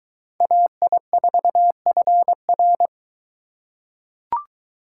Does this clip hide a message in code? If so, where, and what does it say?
Yes: Morse code from 0.4 to 2.9 s, heard as a keyed tone, about 21 wpm, 700 Hz. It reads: AI4FR